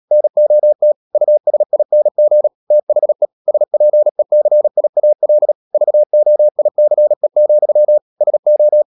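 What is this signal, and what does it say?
Morse code, heard as a keyed tone, about 37 wpm, 600 Hz. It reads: NOT USING THE SPECIAL VOICE, SO
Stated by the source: Morse code practice at 35–39 wpm sending English text of Ken Follett's novel Never, from CW med Gnister.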